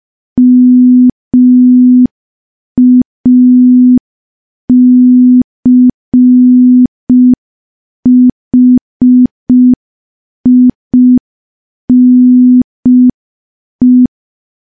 Morse code, heard as a keyed tone, about 5 wpm, 250 Hz. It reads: MACHINE
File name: morse / morse-training-05WPM